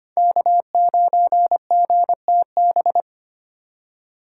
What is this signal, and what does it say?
Morse code, heard as a keyed tone, about 32 wpm, 700 Hz. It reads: K9GTB